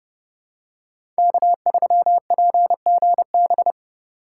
Morse code, heard as a keyed tone, about 30 wpm, 700 Hz. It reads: K3PGB
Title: USA Call Signs - 5 Characters - 4x Character Spacing - 30wpm